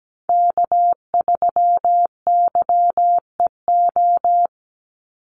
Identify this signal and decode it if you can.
Morse code, heard as a keyed tone, about 17 wpm, 700 Hz. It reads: K3YEO
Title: Call Signs Repeated 2x - 17wpm